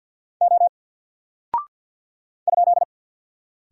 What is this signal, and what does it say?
Morse code, heard as a keyed tone, about 49 wpm, 700 Hz. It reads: O ?
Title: Single Character (Letter:Number:Punctuation:Prosign) Alt 50wpm